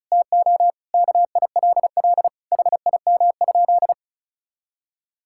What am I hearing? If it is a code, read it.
Morse code, heard as a keyed tone, about 34 wpm, 700 Hz. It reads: TO KILL HIM?
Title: Sentences from Top 1600 Words 35wpm